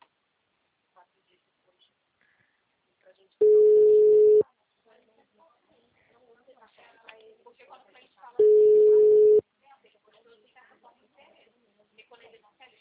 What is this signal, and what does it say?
Ringback tone (Europe, 425 Hz), with speech.
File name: out- -2013- - - .0